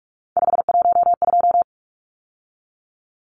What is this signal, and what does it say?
Morse code, heard as a keyed tone, about 44 wpm, 700 Hz. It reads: 512
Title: Numbers - 3-Digit 45wpm